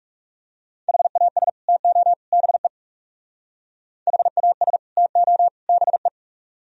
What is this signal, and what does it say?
Morse code, heard as a keyed tone, about 37 wpm, 700 Hz. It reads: HAS TO BE HAS TO BE
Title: Speed-Racing - Sets of 3 Words 30wpm